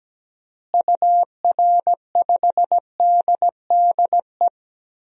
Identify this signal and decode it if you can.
Morse code, heard as a keyed tone, about 17 wpm, 700 Hz. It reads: UR5DDE
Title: DXCC-100 Call Signs - 6 Characters - Repeated 3x - 17wpm